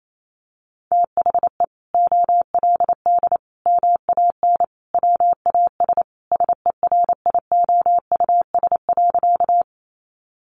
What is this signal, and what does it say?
Morse code, heard as a keyed tone, about 28 wpm, 700 Hz. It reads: THE OLD MAN WAS SERIOUS.
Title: Sentences from Top 1100 Words 28wpm